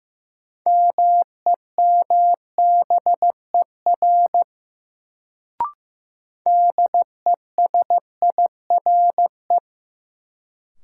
Morse code, heard as a keyed tone, about 15 wpm, 700 Hz. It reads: MEMBER DESIRE
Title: Sentences from Top 1100 Words - New Word Drill - 15wpm